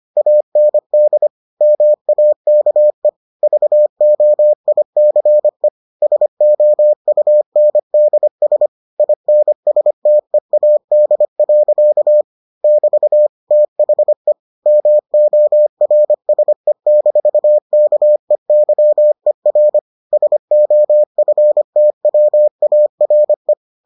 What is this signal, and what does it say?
Morse code, heard as a keyed tone, about 25 wpm, 600 Hz. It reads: AND MAKE VOICE SOUNDS INSTEAD. <BT> THE MORSE-KEYER SOFTWARE